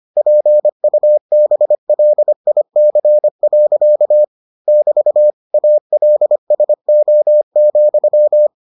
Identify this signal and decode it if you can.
Morse code, heard as a keyed tone, about 25 wpm, 600 Hz. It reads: PUBLIC. <BT> ALSO,